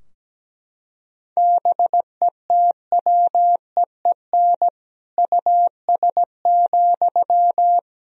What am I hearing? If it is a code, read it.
Morse code, heard as a keyed tone, about 17 wpm, 700 Hz. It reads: BETWEEN US,
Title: Sentences from Top 1000 Words - Review Part 2 - 17wpm